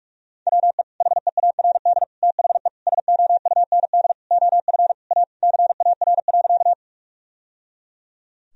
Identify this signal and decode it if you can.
Morse code, heard as a keyed tone, about 46 wpm, 700 Hz. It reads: WE HEARD THE SOUND OF A CAR.
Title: Sentences from Top 1000 Words - Review Part 1 - 45wpm